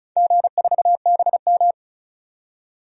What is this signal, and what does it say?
Morse code, heard as a keyed tone, about 34 wpm, 700 Hz. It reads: G4BM
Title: DXCC-100 Call Signs - 4 Characters - 35wpm